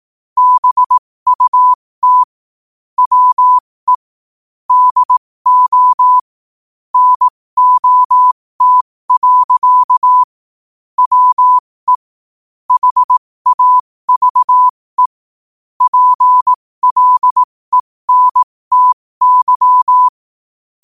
Morse code, heard as a keyed tone, about 18 wpm, spaced as for 15 wpm, 1000 Hz. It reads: BUT WE DO NOT. WE HAVE PLENTY
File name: qotdfiles / 15wpm